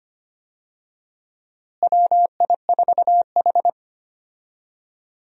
Morse code, heard as a keyed tone, about 25 wpm, 700 Hz. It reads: WI4H
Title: USA Call Signs - 4 Characters - Repeated 2x - 25wpm